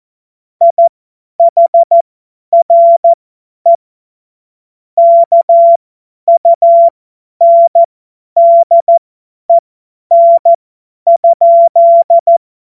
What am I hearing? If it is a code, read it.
Morse code, heard as a keyed tone, about 14 wpm, spaced as for 10 wpm, 680 Hz. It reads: IHRE KUNDEN?
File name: Witz-15-Juli-2025-50-F70